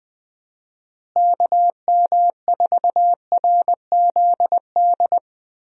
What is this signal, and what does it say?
Morse code, heard as a keyed tone, about 20 wpm, 700 Hz. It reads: KM4RZD
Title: USA Call Signs - 6 Characters - Repeated 2x - 20wpm